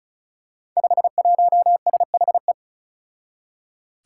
Morse code, heard as a keyed tone, about 34 wpm, 700 Hz. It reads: 51SHE
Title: Five Challenging Letter-Number Combinations 35wpm